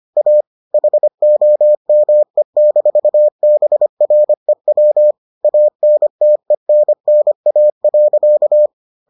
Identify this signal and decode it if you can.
Morse code, heard as a keyed tone, about 25 wpm, 600 Hz. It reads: A HOME-BREW ANTENNA.